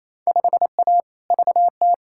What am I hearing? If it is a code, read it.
Morse code, heard as a keyed tone, about 28 wpm, 700 Hz. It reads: 5A VT
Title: Field Day Exchanges Encore - 28wpm